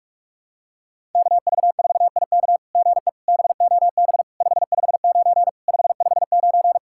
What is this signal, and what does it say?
Morse code, heard as a keyed tone, about 43 wpm, 690 Hz. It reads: KV4IK GE BOB 559 559
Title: SOTA QSO Practice 45wpm